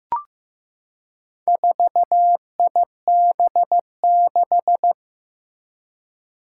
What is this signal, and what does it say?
Morse code, heard as a keyed tone, about 16 wpm, 700 Hz. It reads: 4IB6
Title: Four Challenging Letter-Number Combinations 15wpm